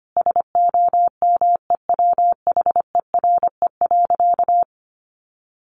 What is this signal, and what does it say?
Morse code, heard as a keyed tone, about 25 wpm, 700 Hz. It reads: SOMEWHERE.
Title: Sentences from Top 1400 Words 25wpm